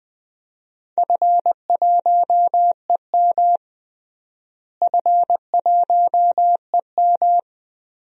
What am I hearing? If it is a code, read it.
Morse code, heard as a keyed tone, about 20 wpm, 700 Hz. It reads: F1EM F1EM
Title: DXCC-100 Call Signs - 4 Characters - Repeated 3x - 20wpm